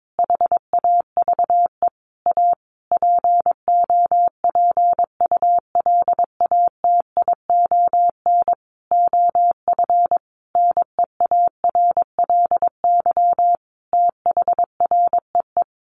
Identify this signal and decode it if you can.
Morse code, heard as a keyed tone, about 22 wpm, 700 Hz. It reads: HAVE A POPULATION OF NEARLY THREE
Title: Sentences from Top 1300 Words 22wpm